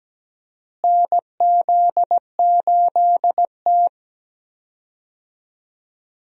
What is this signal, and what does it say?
Morse code, heard as a keyed tone, about 17 wpm, 700 Hz. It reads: NZ8T
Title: USA Call Signs - 4 Characters - Repeated 3x - 17wpm